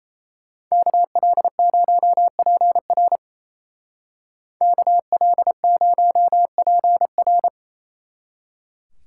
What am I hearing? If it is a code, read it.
Morse code, heard as a keyed tone, about 33 wpm, 700 Hz. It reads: KL0PR KL0PR
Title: Speed Racing - USA 2x2 Call Signs 28wpm